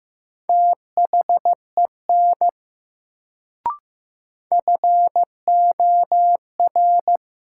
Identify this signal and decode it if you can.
Morse code, heard as a keyed tone, about 16 wpm, 700 Hz. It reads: THEN FOR
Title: Sentences from Top 100 Words - New Word Drill - 15wpm